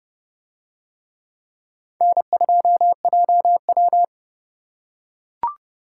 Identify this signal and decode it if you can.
Morse code, heard as a keyed tone, about 29 wpm, 700 Hz. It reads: N2JW